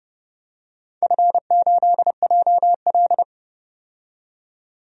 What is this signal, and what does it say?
Morse code, heard as a keyed tone, about 31 wpm, 700 Hz. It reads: F8JL
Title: DXCC-100 Call Signs - 4 Characters - 30wpm